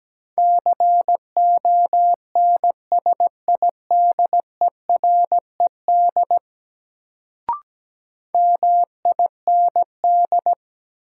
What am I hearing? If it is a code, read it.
Morse code, heard as a keyed tone, about 17 wpm, 700 Hz. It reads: CONSIDERED MIND
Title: Sets of 5 Words - New Word Drill - 17wpm